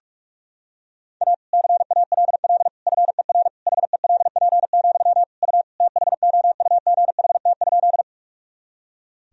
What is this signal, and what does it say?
Morse code, heard as a keyed tone, about 41 wpm, 700 Hz. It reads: A CALL FER HELP, U THOUGHT?
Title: Sentences from Top 300 Words and Abbreviations 45wpm